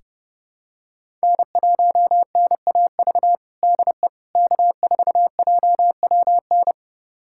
Morse code, heard as a keyed tone, about 32 wpm, 700 Hz. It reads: N1NAV DE K4JWN
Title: USA Callsign DE Callsign - 5 Characters - 30wpm